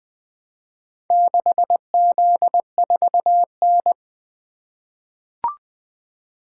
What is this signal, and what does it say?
Morse code, heard as a keyed tone, about 20 wpm, 700 Hz. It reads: 6Z4N
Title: Four Letter-Number Combinations - 20wpm